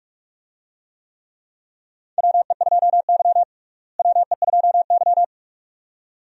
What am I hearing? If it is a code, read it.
Morse code, heard as a keyed tone, about 48 wpm, 700 Hz. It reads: WE2Y WE2Y
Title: USA Call Signs - 4 Characters - Repeated 2x - 45wpm